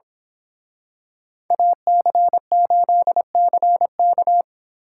Morse code, heard as a keyed tone, about 26 wpm, 700 Hz. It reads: AC8CK
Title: Speed Racing - USA Call Signs 22wpm